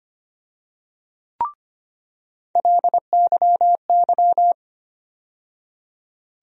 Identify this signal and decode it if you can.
Morse code, heard as a keyed tone, about 25 wpm, 700 Hz. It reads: LYY
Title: USA Call Sign Postfixes - 3 Letters 25wpm